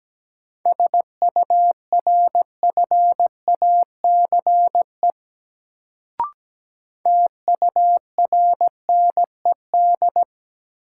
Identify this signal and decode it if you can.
Morse code, heard as a keyed tone, about 17 wpm, 700 Hz. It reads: SURFACE TURNED
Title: Sets of 4 Words - New Word Drill - 17wpm